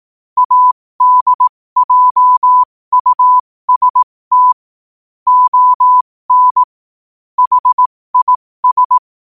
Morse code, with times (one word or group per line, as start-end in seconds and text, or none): 0.4-4.5 ADJUST
5.3-6.6 ON
7.4-9.0 HIS